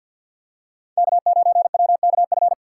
Morse code, heard as a keyed tone, about 52 wpm, 690 Hz. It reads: K9WKF
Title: USA Call Signs - 5 Characters - 3x Character Spacing - 50wpm